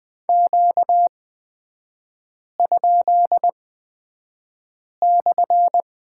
Morse code, heard as a keyed tone, about 20 wpm, spaced as for 5 wpm, 700 Hz. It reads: Q?/
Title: 236 Lesson 38e - Letter q - Three Characters 20wpm